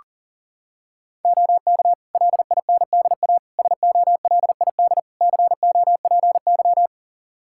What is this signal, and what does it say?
Morse code, heard as a keyed tone, about 37 wpm, 700 Hz. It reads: OK LINDA SOLID COPY
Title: Practice QSO - Exchange 3 - More Info - 40wpm